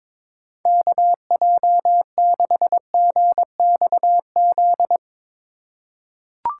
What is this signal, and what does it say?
Morse code, heard as a keyed tone, about 22 wpm, 700 Hz. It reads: KJ6GXZ
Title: USA Call Signs - 6 Characters - 4x Character Spacing - 22wpm